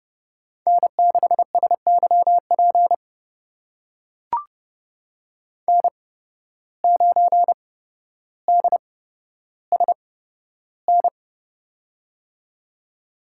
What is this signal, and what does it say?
Morse code, heard as a keyed tone, about 32 wpm, spaced as for 13 wpm, 700 Hz. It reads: N6SYP N9DSN
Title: USA Call Signs - 8x Character Spacing - 30wpm